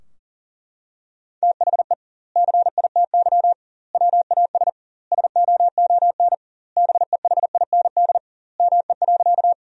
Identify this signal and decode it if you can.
Morse code, heard as a keyed tone, about 40 wpm, 700 Hz. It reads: THE CITY WAS SOON BEHIND ME.
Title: 2x Word-Spacing - Sentences from Top 500 Words - 40wpm